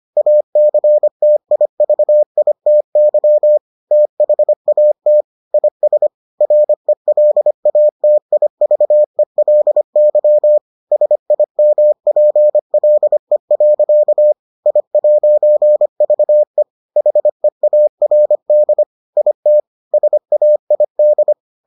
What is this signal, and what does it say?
Morse code, heard as a keyed tone, about 25 wpm, 600 Hz. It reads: ACTIVITY THAT IS RELATIVELY SIMPLE. I'VE HEARD IT SAID